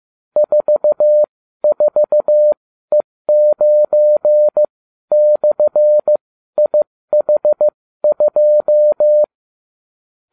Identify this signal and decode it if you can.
Morse code, heard as a keyed tone, about 15 wpm, spaced as for 12 wpm, 600 Hz. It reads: 44E9/IH2